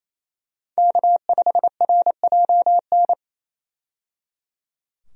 Morse code, heard as a keyed tone, about 27 wpm, 700 Hz. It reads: K5RJN
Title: USA Call Signs - 5 Characters - 28wpm